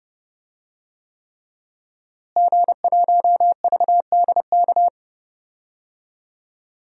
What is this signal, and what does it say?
Morse code, heard as a keyed tone, about 30 wpm, 700 Hz. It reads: G1VDK
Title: DXCC-100 Call Signs - 5 Characters - Repeated 3x - 30wpm